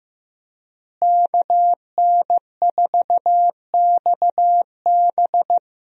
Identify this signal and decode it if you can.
Morse code, heard as a keyed tone, about 15 wpm, 700 Hz. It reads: KN4XB
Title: USA Call Signs - 5 Characters - Repeated 3x - 15wpm